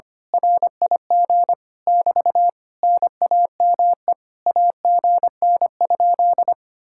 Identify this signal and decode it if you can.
Morse code, heard as a keyed tone, about 24 wpm, 700 Hz. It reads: RIG <BT> NAME AGN?
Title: Speed-Racing Common QSO Phrases - 25wpm